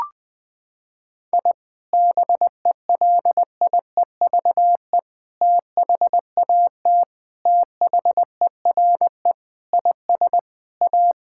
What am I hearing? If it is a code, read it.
Morse code, heard as a keyed tone, about 20 wpm, 700 Hz. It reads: I BELIEVE THAT THERE IS A
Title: Sentences with 4-Word Phrases - 20wpm